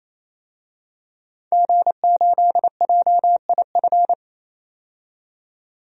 Morse code, heard as a keyed tone, about 28 wpm, 700 Hz. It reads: G8JIF
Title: DXCC-100 Call Signs - 5 Characters - 28wpm